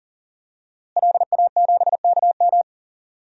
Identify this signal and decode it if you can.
Morse code, heard as a keyed tone, about 40 wpm, 700 Hz. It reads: LA7KM
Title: DXCC-100 Call Signs - 5 Characters - Repeated 2x - 40wpm